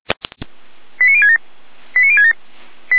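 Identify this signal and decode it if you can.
Five-tone selective call (ZVEI), heard as quick runs of five tones, tones of about 70 ms. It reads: 89067 89067
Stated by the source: ZVEI five-tone selective call received on SDR.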